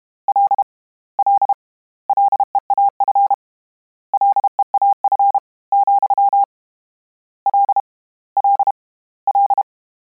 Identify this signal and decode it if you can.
Morse code, heard as a keyed tone, about 32 wpm, 800 Hz. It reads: L L LEAF LEAF, L L L
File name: Combinations / Combinations-mp3s / Combinations@30